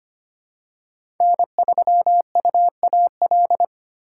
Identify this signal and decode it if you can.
Morse code, heard as a keyed tone, about 25 wpm, 700 Hz. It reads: N3UAL
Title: USA Call Signs - 5 Characters - 2x Character Spacing - 25wpm